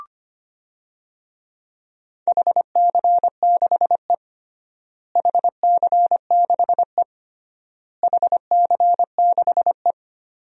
Morse code, heard as a keyed tone, about 25 wpm, 700 Hz. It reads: HC6E HC6E HC6E
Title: DXCC-100 Call Signs - 4 Characters - Repeated 3x - 25wpm